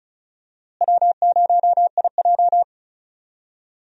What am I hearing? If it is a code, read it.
Morse code, heard as a keyed tone, about 36 wpm, 700 Hz. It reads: W0IJ
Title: USA Call Signs - 4 Characters - Repeated 2x - 35wpm